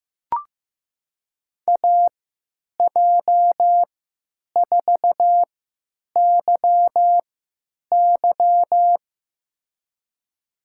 Morse code, heard as a keyed tone, about 15 wpm, spaced as for 9 wpm, 700 Hz. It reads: AJ4YY
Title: USA Call Signs - 5 Characters - 3x Character Spacing - 15wpm